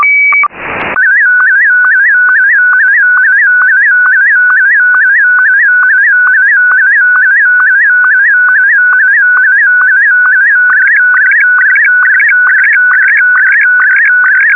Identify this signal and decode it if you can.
SSTV (slow-scan television), Martin 1.
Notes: transmitting and receiving SSTV signals is pretty easy, using just a raspberry pi as a transmitter and an RTL-SDR as a receiver.